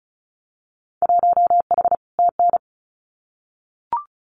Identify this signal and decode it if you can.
Morse code, heard as a keyed tone, about 36 wpm, 700 Hz. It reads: 1H TN
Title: Winter Field Day Exchanges 35wpm